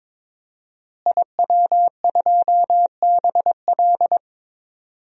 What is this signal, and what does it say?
Morse code, heard as a keyed tone, about 22 wpm, 700 Hz. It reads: IW2BL
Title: DXCC-100 Call Signs - 5 Characters - Repeated 3x - 22wpm